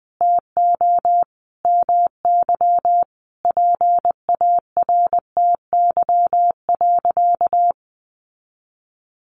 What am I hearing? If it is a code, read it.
Morse code, heard as a keyed tone, about 20 wpm, 700 Hz. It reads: TO MY PARTY.
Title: Sentences from Top 1400 Words 20wpm